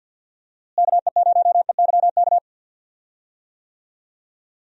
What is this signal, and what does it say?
Morse code, heard as a keyed tone, about 53 wpm, 700 Hz. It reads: KE0EYK
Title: USA Call Signs - 6 Characters - Repeated 3x - 50wpm